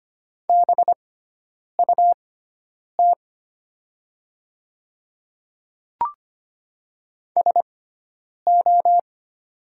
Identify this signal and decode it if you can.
Morse code, heard as a keyed tone, about 25 wpm, spaced as for 9 wpm, 700 Hz. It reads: BUT SO
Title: 6x Character Spacing - Top 100 Words - 25wpm